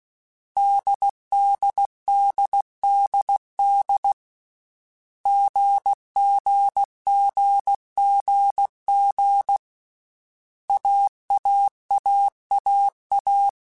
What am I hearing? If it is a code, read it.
Morse code, heard as a keyed tone, about 16 wpm, 800 Hz. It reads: DDDDD GGGGG AAAAA
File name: Basic Sets / Basic Letter Groups Collective7 / AWUNDG@13